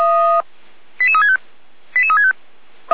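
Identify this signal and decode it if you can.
Five-tone selective call (ZVEI), heard as quick runs of five tones, tones of about 80 ms. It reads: 88365 88365
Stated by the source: ZVEI five-tone selective call received on SDR.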